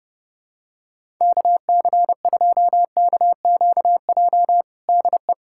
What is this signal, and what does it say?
Morse code, heard as a keyed tone, about 30 wpm, 700 Hz. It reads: KC2KQJ DE
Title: USA Callsign DE Callsign - 6 Characters - 30wpm